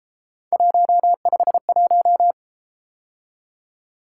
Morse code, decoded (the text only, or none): 151